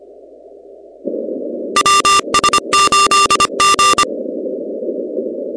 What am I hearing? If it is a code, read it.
Morse code, heard as a keyed tone, about 25 wpm, 1190 Hz, with noise. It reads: WS8G
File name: contests / 2019-cw-wpx-kt5j-k5pi- -2330 0-16